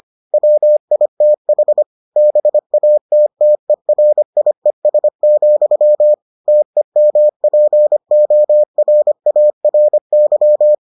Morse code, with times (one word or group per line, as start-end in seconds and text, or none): 0.3-1.8 WITH
2.2-6.1 BATTERIES,
6.5-10.7 TEMPORARY